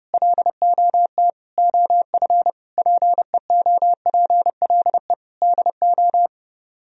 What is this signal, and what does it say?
Morse code, heard as a keyed tone, about 33 wpm, 700 Hz. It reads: LOT OF PEOPLE DO